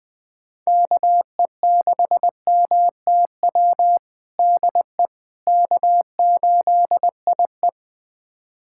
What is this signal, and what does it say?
Morse code, heard as a keyed tone, about 19 wpm, 700 Hz. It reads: KE6MTW DE K8IE
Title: USA Callsign DE Callsign 20wpm